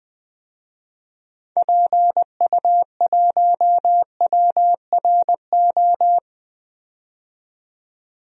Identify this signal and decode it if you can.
Morse code, heard as a keyed tone, about 20 wpm, 700 Hz. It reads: PU1WRO
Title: DXCC-100 Call Signs - 6 Characters - 20wpm